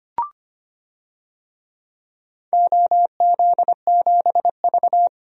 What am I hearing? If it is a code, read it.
Morse code, heard as a keyed tone, about 25 wpm, 700 Hz. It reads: OZ7V
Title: DXCC-100 Call Signs - 4 Characters - Repeated 3x - 25wpm